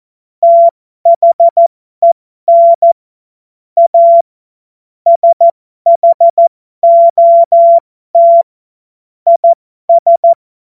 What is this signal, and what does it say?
Morse code, heard as a keyed tone, about 14 wpm, spaced as for 12 wpm, 680 Hz. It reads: THEN A SHOT IS
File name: Morsekurs / Witze